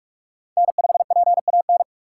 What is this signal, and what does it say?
Morse code, heard as a keyed tone, about 43 wpm, 700 Hz. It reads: N5PAN